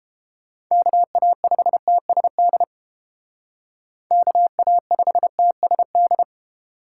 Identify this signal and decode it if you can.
Morse code, heard as a keyed tone, about 31 wpm, 700 Hz. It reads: KA5TSD KA5TSD